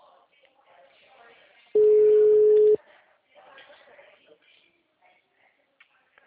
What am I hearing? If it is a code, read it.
Ringback tone (Europe, 425 Hz), with speech.